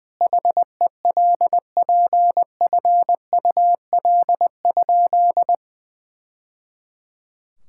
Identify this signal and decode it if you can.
Morse code, heard as a keyed tone, about 20 wpm, 700 Hz. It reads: HELPFUL?
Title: Sentences from Top 1000 Words - Review Part 6 - 20wpm